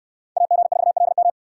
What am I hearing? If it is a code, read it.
Morse code, heard as a keyed tone, about 69 wpm, 700 Hz. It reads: AG4LM